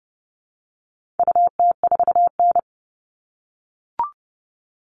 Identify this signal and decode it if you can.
Morse code, heard as a keyed tone, about 30 wpm, 700 Hz. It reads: UT4N
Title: All Cut Numbers - 4-Digit 30wpm